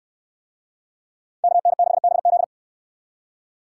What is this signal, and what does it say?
Morse code, heard as a keyed tone, about 64 wpm, 700 Hz. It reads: KN6BZ